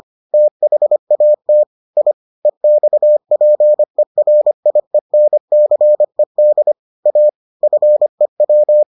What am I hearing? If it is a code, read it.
Morse code, heard as a keyed tone, about 25 wpm, 600 Hz. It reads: THAT I EXPERIENCED A FEW